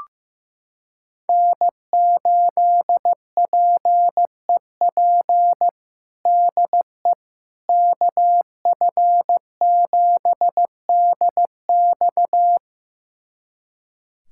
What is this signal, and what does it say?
Morse code, heard as a keyed tone, about 16 wpm, 700 Hz. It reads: N8PEP DE KF7DX
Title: USA Callsign DE Callsign - 5 Characters - 15wpm